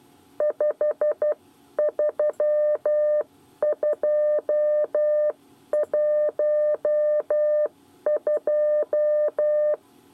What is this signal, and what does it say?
Morse code, heard as a keyed tone, about 12 wpm, spaced as for 10 wpm, 600 Hz. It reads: 53212
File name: 8 bis lezione cw 1